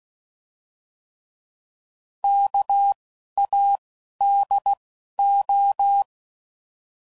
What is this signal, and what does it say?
Morse code, heard as a keyed tone, about 16 wpm, 800 Hz. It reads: K A D O